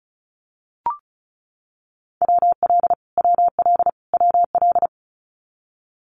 Morse code, heard as a keyed tone, about 34 wpm, 700 Hz. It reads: WL WL WL
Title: Abbreviations Repeated 3x - 35wpm